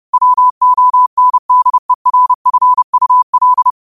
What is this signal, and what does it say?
Morse code, heard as a keyed tone, about 30 wpm, 1000 Hz. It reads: WONDERFUL
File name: qotdfiles / 30wpm